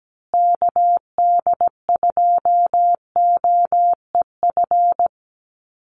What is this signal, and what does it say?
Morse code, heard as a keyed tone, about 17 wpm, 700 Hz. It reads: KD2OEF
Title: USA Call Signs Repeated 3x 17wpm